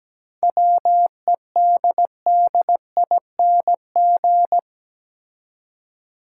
Morse code, heard as a keyed tone, about 17 wpm, 700 Hz. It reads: WEDDING
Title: Seven Letter Words - 17wpm